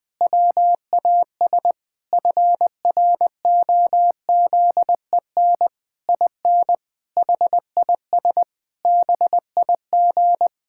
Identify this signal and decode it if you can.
Morse code, heard as a keyed tone, about 20 wpm, 700 Hz. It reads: WAS FROZEN IN HIS BIG
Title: Pangram Challenge v2 - 20wpm